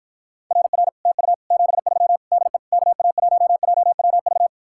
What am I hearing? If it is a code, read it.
Morse code, heard as a keyed tone, about 54 wpm, 700 Hz. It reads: RR TU 73 DE KA1JWV